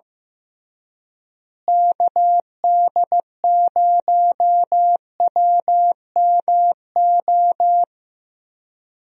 Morse code, heard as a keyed tone, about 15 wpm, 700 Hz. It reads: KD0WMO
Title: USA Call Signs - 6 Characters - Repeated 3x - 15wpm